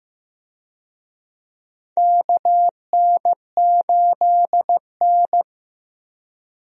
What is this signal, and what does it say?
Morse code, heard as a keyed tone, about 15 wpm, 700 Hz. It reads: KN8N
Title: USA Call Signs - 4 Characters - Repeated 2x - 15wpm